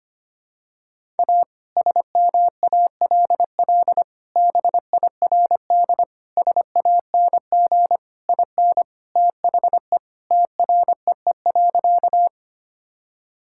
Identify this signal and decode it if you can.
Morse code, heard as a keyed tone, about 25 wpm, 700 Hz. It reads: A SMALL BIRD SANG IN THE TREE.
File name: practice / 25wpm / Sentences with 2-Word Phrases - 25wpm